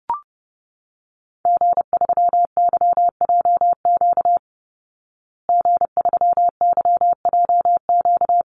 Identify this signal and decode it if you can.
Morse code, heard as a keyed tone, about 30 wpm, 700 Hz. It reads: G3YJQ G3YJQ
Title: Call Signs Repeated 2x - 30wpm